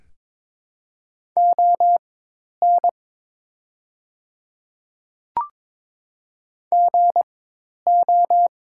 Morse code, heard as a keyed tone, about 22 wpm, spaced as for 10 wpm, 700 Hz. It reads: ON GO
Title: 4x Character Spacing - Top 100 Words - 22wpm